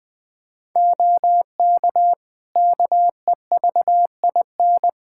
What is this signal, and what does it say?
Morse code, heard as a keyed tone, about 20 wpm, 700 Hz. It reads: OK KEVIN
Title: Practice QSO - Exchange 3 - More Info - 20wpm